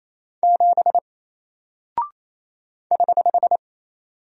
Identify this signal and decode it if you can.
Morse code, heard as a keyed tone, about 28 wpm, 700 Hz. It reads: 7 <HH>
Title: Single Character (Letter:Number:Punctuation:Prosign) Alt 28wpm